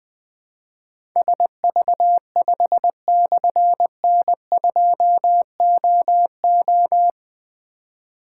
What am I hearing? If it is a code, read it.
Morse code, heard as a keyed tone, about 20 wpm, 700 Hz. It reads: SV5/N2OO
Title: Difficult Call Signs 20wpm